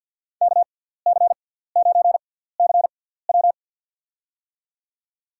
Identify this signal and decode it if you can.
Morse code, heard as a keyed tone, about 50 wpm, spaced as for 17 wpm, 700 Hz. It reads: KC9CW
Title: USA Call Signs - 6x Character Spacing - 50wpm